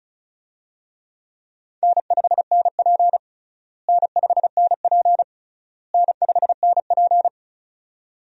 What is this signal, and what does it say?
Morse code, heard as a keyed tone, about 36 wpm, 700 Hz. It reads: N5NP N5NP N5NP
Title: USA Call Signs - 4 Characters - Repeated 3x - 35wpm